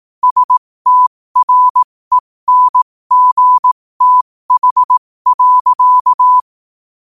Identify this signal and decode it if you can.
Morse code, heard as a keyed tone, about 18 wpm, spaced as for 15 wpm, 1000 Hz. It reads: STRENGTH.
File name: qotdfiles / 15wpm